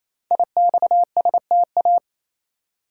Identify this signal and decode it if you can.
Morse code, heard as a keyed tone, about 28 wpm, 700 Hz. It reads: IXSTA
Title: Code Groups - 5 Letters - 28wpm